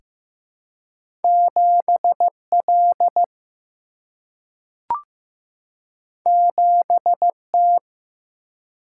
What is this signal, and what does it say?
Morse code, heard as a keyed tone, about 15 wpm, 700 Hz. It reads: 7 L 7 T
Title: Single-Letter Single-Number 15wpm